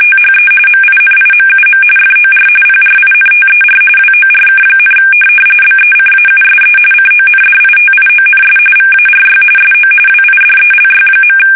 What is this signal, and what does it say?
RTTY, about 45.45 baud. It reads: THE MINIMUM FOR PRINTER WORK, BECAUSE CONDITIONS VARY SO WIDELY.
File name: fsk850